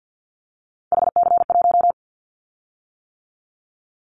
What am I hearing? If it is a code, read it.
Morse code, heard as a keyed tone, about 50 wpm, 690 Hz. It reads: HC1
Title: Callsign Prefixes for Top 100 DXCC - 50wpm